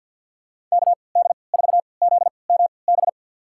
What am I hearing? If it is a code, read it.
Morse code, heard as a keyed tone, about 47 wpm, spaced as for 28 wpm, 700 Hz. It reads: KD4ZMB
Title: USA Call Signs - 3x Character Spacing - 50wpm